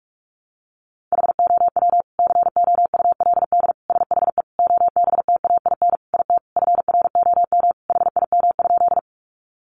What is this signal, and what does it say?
Morse code, heard as a keyed tone, about 44 wpm, 700 Hz. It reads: HOW COULD SHE OBTAIN IT FROM HIM?
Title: Sentences from Top 1300 Words 45wpm